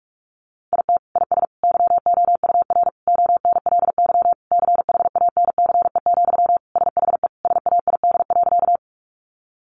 Morse code, heard as a keyed tone, about 44 wpm, 700 Hz. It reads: IT IS YOUR ONLY CHANCE, SHE SAID.
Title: Sentences from Top 800 Words 45wpm